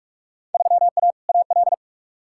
Morse code, heard as a keyed tone, about 43 wpm, 700 Hz. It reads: 3A AL
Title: Field Day Exchanges - 45wpm